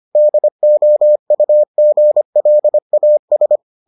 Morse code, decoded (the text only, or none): DOUGLAS